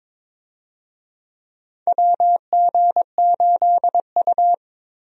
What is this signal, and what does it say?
Morse code, heard as a keyed tone, about 22 wpm, 700 Hz. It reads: WG8U
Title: USA Call Signs - 4 Characters - Repeated 2x - 22wpm